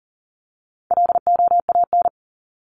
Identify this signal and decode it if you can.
Morse code, heard as a keyed tone, about 40 wpm, 700 Hz. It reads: LOAN
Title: Top 1501-1750 Words 40wpm